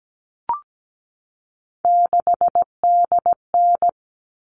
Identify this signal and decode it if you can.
Morse code, heard as a keyed tone, about 17 wpm, 700 Hz. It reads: 6DN
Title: All Cut Numbers - 3-Digit 17wpm